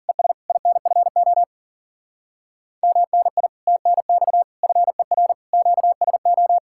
Morse code, heard as a keyed tone, about 51 wpm, 700 Hz. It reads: ES INFO MNI TNX FER QSO